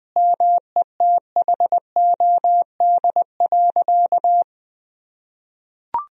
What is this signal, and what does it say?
Morse code, heard as a keyed tone, about 20 wpm, 700 Hz. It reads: METHOD.
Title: Sentences from Top 1000 Words - Review Part 3 - 20wpm